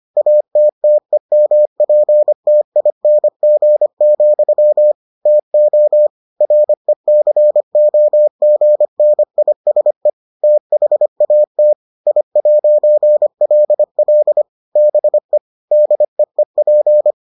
Morse code, heard as a keyed tone, about 25 wpm, 600 Hz. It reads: ATTEMPTING, TO RECOGNISE THAT I'LL BE DEEP